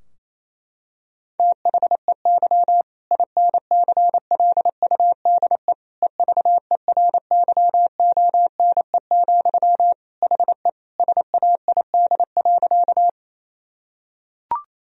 Morse code, heard as a keyed tone, about 28 wpm, 700 Hz. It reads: THEY INCLUDE EVERYONE, HE SAID.
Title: Sentences from Top 1000 Words - Review Part 3 - 28wpm